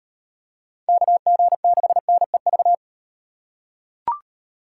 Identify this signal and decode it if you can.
Morse code, heard as a keyed tone, about 38 wpm, 700 Hz. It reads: KG6NEV